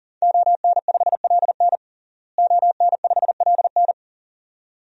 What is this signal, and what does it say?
Morse code, heard as a keyed tone, about 40 wpm, 700 Hz. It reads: ON5LN ON5LN
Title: DXCC-100 Call Signs - 5 Characters - Repeated 2x - 40wpm